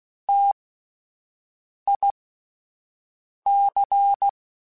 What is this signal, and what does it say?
Morse code, heard as a keyed tone, about 16 wpm, spaced as for 5 wpm, 800 Hz. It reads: TIC
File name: T Prefixes / T mp3s